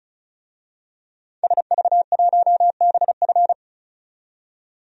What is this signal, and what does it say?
Morse code, heard as a keyed tone, about 34 wpm, 700 Hz. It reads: SV1BF
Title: DXCC-100 Call Signs - 5 Characters - 35wpm